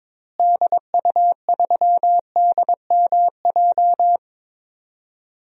Morse code, heard as a keyed tone, about 22 wpm, 700 Hz. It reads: DU3DMJ
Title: DXCC-100 Call Signs - 6 Characters - Repeated 2x - 22wpm